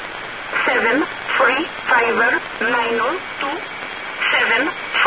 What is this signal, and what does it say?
Emergency Alert System (SAME) tones, heard as harsh buzzing bursts, with speech. It is heard from 0.5 to 4.6 s.